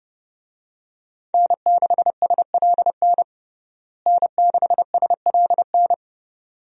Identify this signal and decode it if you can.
Morse code, heard as a keyed tone, about 30 wpm, 700 Hz. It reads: N6SLN N6SLN
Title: USA Call Signs - 5 Characters - Repeated 2x - 30wpm